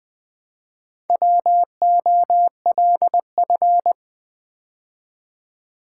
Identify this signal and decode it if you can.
Morse code, heard as a keyed tone, about 20 wpm, 700 Hz. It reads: WOLF
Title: Build A Word - 2 Letters At A Time - 4-Letter Words - 20wpm